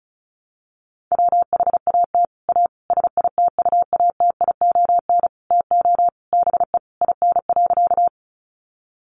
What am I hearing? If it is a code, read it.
Morse code, heard as a keyed tone, about 34 wpm, 700 Hz. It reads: WHAT A SITUATION TO BE IN.
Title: Sentences from Top 1200 Words 35wpm